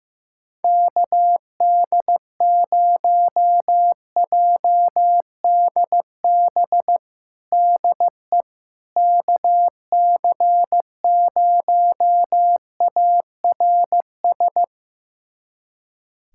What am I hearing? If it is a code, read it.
Morse code, heard as a keyed tone, about 15 wpm, 700 Hz. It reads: KD0JDB DE KC0ARS
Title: USA Callsign DE Callsign - 6 Characters - 15wpm